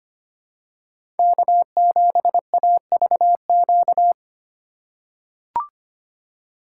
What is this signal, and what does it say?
Morse code, heard as a keyed tone, about 25 wpm, 700 Hz. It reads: K7AVQ
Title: USA Call Signs - 5 Characters - 2x Character Spacing - 25wpm